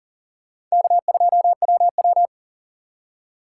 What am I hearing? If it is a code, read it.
Morse code, heard as a keyed tone, about 40 wpm, 700 Hz. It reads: K2WW
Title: USA Call Signs - 2x Character Spacing - 40wpm